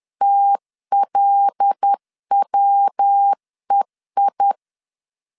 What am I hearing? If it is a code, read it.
Morse code, heard as a keyed tone, about 10 wpm, 800 Hz. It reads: TLWEI
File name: LEZIONI / 6 PKX0 / 6 PKX0 10wpm 2707